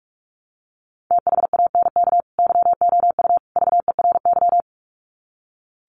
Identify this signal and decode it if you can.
Morse code, heard as a keyed tone, about 44 wpm, 700 Hz. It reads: THANK YOU VERY
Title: Sets of 3 Words 45wpm